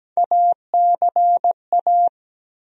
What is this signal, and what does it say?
Morse code, heard as a keyed tone, about 17 wpm, 700 Hz. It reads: ACA
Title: USA Call Sign Postfixes - 3 Letters 17wpm